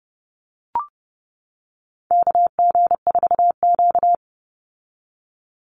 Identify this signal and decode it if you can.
Morse code, heard as a keyed tone, about 30 wpm, 700 Hz. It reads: KG4Q
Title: Call Signs - 30wpm